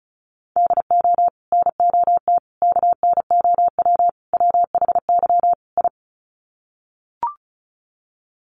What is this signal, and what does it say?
Morse code, heard as a keyed tone, about 34 wpm, 700 Hz. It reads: DO NOT KNOW WHY I
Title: Sets of 5 Words Encore 35wpm